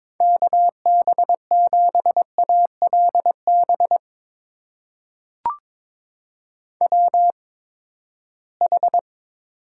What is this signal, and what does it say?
Morse code, heard as a keyed tone, about 22 wpm, spaced as for 15 wpm, 700 Hz. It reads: KB7ALB WH
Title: USA Call Signs - 6 Characters - 8x Character Spacing - 22wpm